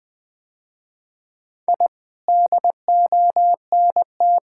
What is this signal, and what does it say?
Morse code, heard as a keyed tone, about 20 wpm, 700 Hz. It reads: I DONT
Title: Sentences with 8-Word Phrases - Spoken Beginning - 20wpm